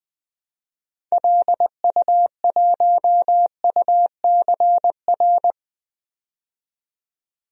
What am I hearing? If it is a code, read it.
Morse code, heard as a keyed tone, about 20 wpm, 700 Hz. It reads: LU1UCR
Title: DXCC-100 Call Signs - 6 Characters - 20wpm